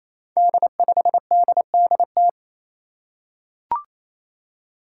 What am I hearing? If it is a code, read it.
Morse code, heard as a keyed tone, about 28 wpm, 700 Hz. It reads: D5DDT
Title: Five Challenging Letter-Number Combinations 28wpm